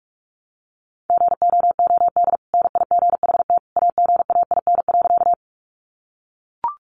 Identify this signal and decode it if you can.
Morse code, heard as a keyed tone, about 44 wpm, 700 Hz. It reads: GOOD NIGHT AGAIN.
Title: Sentences from Top 300 Words 45wpm